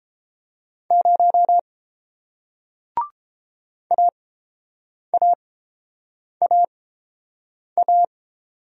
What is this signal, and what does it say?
Morse code, heard as a keyed tone, about 30 wpm, spaced as for 7 wpm, 700 Hz. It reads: 0 AAAA